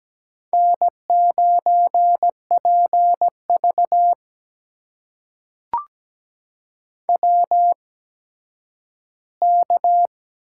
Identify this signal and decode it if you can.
Morse code, heard as a keyed tone, about 17 wpm, spaced as for 10 wpm, 700 Hz. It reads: N9PV WK
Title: USA Call Signs - 4 Characters - 8x Character Spacing - 17wpm